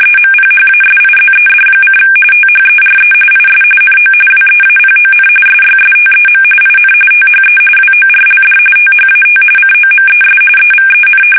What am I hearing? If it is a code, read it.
RTTY, about 45.45 baud. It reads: EMPHASIZED: THERE IS NO WEAPON SYSTEM IN THE WORLD THAT COMES EVEN
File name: fsk850 / battleshipnj